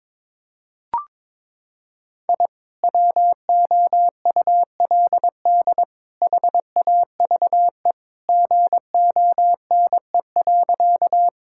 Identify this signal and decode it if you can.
Morse code, heard as a keyed tone, about 22 wpm, 700 Hz. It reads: I WOULD HAVE GONE.
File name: practice / 22wpm / Sentences with 3-Word Phrases - 22wpm